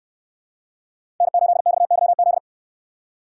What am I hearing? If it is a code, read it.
Morse code, heard as a keyed tone, about 69 wpm, 700 Hz. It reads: N8XYZ